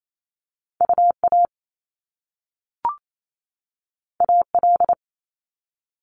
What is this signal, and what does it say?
Morse code, heard as a keyed tone, about 28 wpm, 700 Hz. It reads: UA AL
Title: Top Two Letter Combinations in Words 28wpm